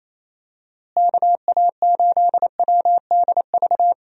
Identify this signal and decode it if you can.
Morse code, heard as a keyed tone, about 27 wpm, 700 Hz. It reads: KA8WDV
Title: USA Call Signs - 6 Characters - Repeated 3x - 28wpm